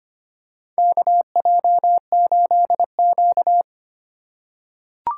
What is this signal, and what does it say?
Morse code, heard as a keyed tone, about 25 wpm, 700 Hz. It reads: KJ8Q
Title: USA Call Signs - 3x Character Spacing - 25wpm